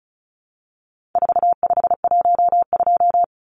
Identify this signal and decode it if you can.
Morse code, heard as a keyed tone, about 34 wpm, 700 Hz. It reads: 4512
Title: Numbers - 4-Digit 35wpm